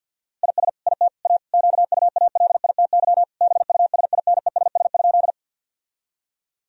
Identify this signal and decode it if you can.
Morse code, heard as a keyed tone, about 51 wpm, 700 Hz. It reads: IS IT A QUALITY BUSINESS?
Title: Sentences from Top 1000 Words - Review Part 4 - 50wpm